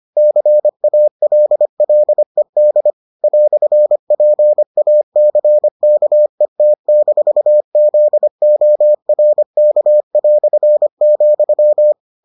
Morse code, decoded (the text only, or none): CALLED "PACKET-ZORK",